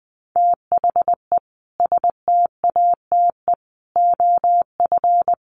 Morse code, heard as a keyed tone, about 20 wpm, 700 Hz. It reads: THE STATE OF
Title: Sets of 5 Words Encore 20wpm